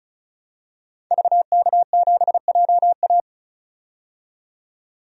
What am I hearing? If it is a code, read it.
Morse code, heard as a keyed tone, about 34 wpm, 700 Hz. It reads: VK7JA
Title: DXCC-100 Call Signs - 5 Characters - 35wpm